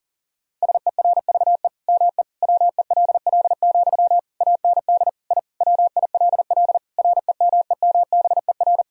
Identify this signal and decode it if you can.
Morse code, heard as a keyed tone, about 40 wpm, 700 Hz. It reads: SERVE ME WELL, AND I WILL REMEMBER
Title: Sentences from Top 1000 Words - Review Part 2 - 40wpm